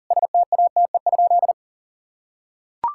Morse code, heard as a keyed tone, about 40 wpm, 700 Hz. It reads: STATE?
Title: Sentences from Top 1000 Words - Review Part 5 - 40wpm